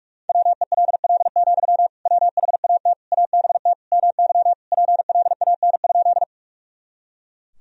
Morse code, heard as a keyed tone, about 43 wpm, 700 Hz. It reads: WELL, WHAT ABT MY PLAN?
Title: Sentences from Top 500 Words and Abbreviations 45wpm